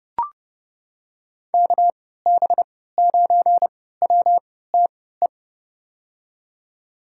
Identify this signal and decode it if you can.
Morse code, heard as a keyed tone, about 32 wpm, spaced as for 17 wpm, 700 Hz. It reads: KB9WTE
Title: USA Call Signs - 6 Characters - 3x Character Spacing - 30wpm